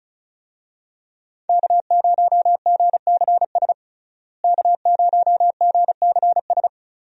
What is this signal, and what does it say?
Morse code, heard as a keyed tone, about 35 wpm, 700 Hz. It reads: K0GCS K0GCS
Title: USA Call Signs - 5 Characters - Repeated 2x - 35wpm